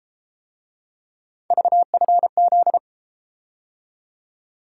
Morse code, heard as a keyed tone, about 31 wpm, 700 Hz. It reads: VFZ